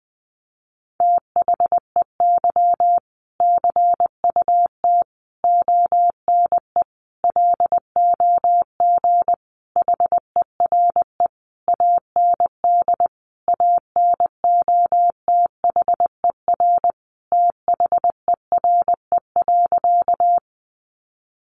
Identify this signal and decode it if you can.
Morse code, heard as a keyed tone, about 20 wpm, 700 Hz. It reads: THEY CUT ONE LOG HERE AND ANOTHER THERE.
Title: Sentences from Top 1400 Words 20wpm